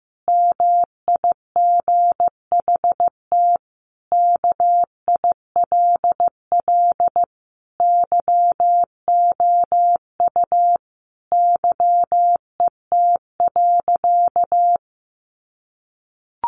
Morse code, heard as a keyed tone, about 15 wpm, 700 Hz. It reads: MIGHT KILL YOU YET.
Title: A Sound Of Thunder 15wpm